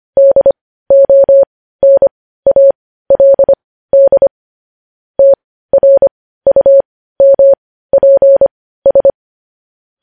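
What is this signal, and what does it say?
Morse code, heard as a keyed tone, about 25 wpm, spaced as for 15 wpm, 550 Hz. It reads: DONALD TRUMPS
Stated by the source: Morse code news headlines sent at 15, preceded by vvv vvv.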